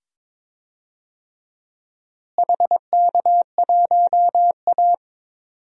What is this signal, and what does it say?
Morse code, heard as a keyed tone, about 22 wpm, 700 Hz. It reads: HK1A
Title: DXCC-100 Call Signs - 4 Characters - Repeated 2x - 22wpm